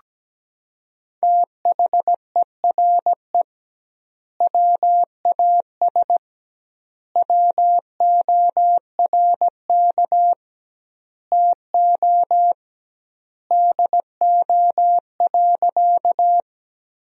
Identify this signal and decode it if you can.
Morse code, heard as a keyed tone, about 17 wpm, 700 Hz. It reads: THERE WAS WORK TO DO.
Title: 2x Word-Spacing - Sentences from Top 200 Words - 17wpm